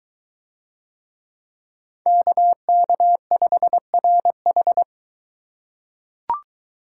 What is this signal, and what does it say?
Morse code, heard as a keyed tone, about 23 wpm, 700 Hz. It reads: KK5RH